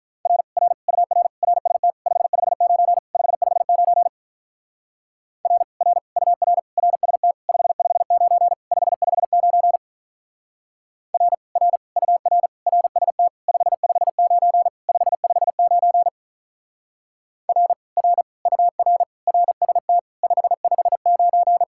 Morse code, heard as a keyed tone, about 46 wpm, 700 Hz. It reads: R R UR RST 559 559 R R UR RST 559 559 R R UR RST 559 559 R R UR RST 559